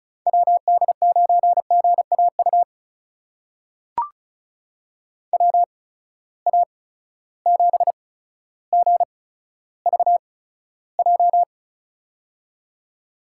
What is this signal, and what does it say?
Morse code, heard as a keyed tone, about 36 wpm, spaced as for 15 wpm, 700 Hz. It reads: WD9GAU WA7GVJ
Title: USA Call Signs - 6 Characters - 8x Character Spacing - 35wpm